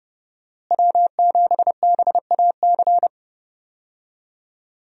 Morse code, heard as a keyed tone, about 28 wpm, 700 Hz. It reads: W7BAC
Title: USA Call Signs - 5 Characters - Repeated 3x - 30wpm